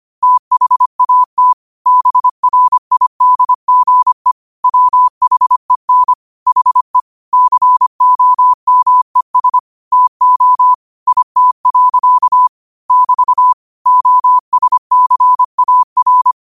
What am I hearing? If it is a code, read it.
Morse code, heard as a keyed tone, about 25 wpm, 1000 Hz. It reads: THAT BRIDGE WHEN HE COMES TO IT. <BT> OSCAR
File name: qotdfiles / 25wpm